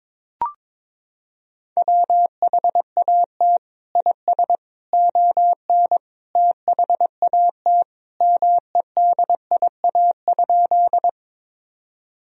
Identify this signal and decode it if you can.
Morse code, heard as a keyed tone, about 22 wpm, 700 Hz. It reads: WHAT IS ON THAT MEDIA?
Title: Sentences from Top 1000 Words - Review Part 4 - 22wpm